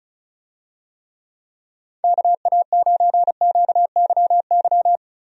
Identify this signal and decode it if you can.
Morse code, heard as a keyed tone, about 36 wpm, 700 Hz. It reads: KA9QYY
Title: USA Call Signs - 6 Characters - Repeated 2x - 35wpm